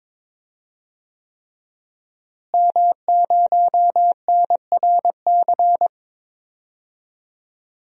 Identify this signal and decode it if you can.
Morse code, heard as a keyed tone, about 22 wpm, 700 Hz. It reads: M0NRC
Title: DXCC-100 Call Signs - 5 Characters - Repeated 3x - 22wpm